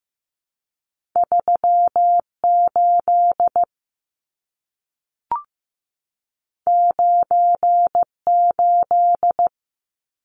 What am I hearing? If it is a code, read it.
Morse code, heard as a keyed tone, about 15 wpm, 700 Hz. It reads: 38 98
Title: Numbers - 2-Digit 15wpm